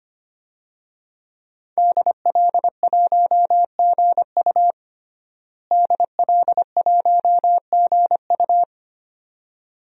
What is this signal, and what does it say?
Morse code, heard as a keyed tone, about 25 wpm, 700 Hz. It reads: DL1GU DL1GU
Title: DXCC-100 Call Signs - 5 Characters - Repeated 2x - 25wpm